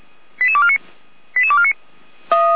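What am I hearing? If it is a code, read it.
Five-tone selective call (ZVEI), heard as quick runs of five tones, tones of about 70 ms. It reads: 88249 88249
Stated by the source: ZVEI five-tone call 88352 received by SDR.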